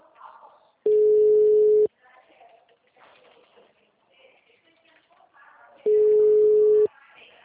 Ringback tone (Europe, 425 Hz).